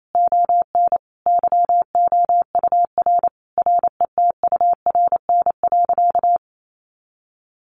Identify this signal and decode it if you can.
Morse code, heard as a keyed tone, about 28 wpm, 700 Hz. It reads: ON YOUR RETURN.
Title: A Sound Of Thunder 28wpm